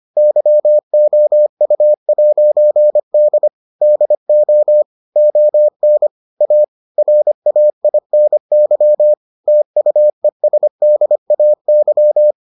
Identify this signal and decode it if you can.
Morse code, heard as a keyed tone, about 25 wpm, 600 Hz. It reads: YOU'D DO ON A RAINY TUESDAY